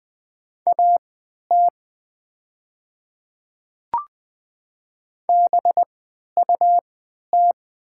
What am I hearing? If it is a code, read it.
Morse code, heard as a keyed tone, about 20 wpm, spaced as for 11 wpm, 700 Hz. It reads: AT BUT
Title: 3x Character Spacing - Top 100 Words - 20wpm